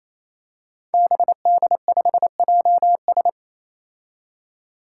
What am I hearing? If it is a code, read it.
Morse code, heard as a keyed tone, about 28 wpm, 700 Hz. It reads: BD5JS
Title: Five Challenging Letter-Number Combinations 28wpm